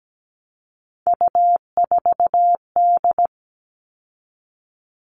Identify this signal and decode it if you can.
Morse code, heard as a keyed tone, about 17 wpm, 700 Hz. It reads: U4D
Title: All Cut Numbers - 3-Digit 17wpm